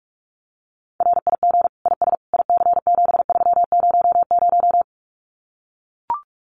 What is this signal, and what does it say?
Morse code, heard as a keyed tone, about 44 wpm, 700 Hz. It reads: RIG IS IC7300
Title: Common QSO Phrases Encore 45wpm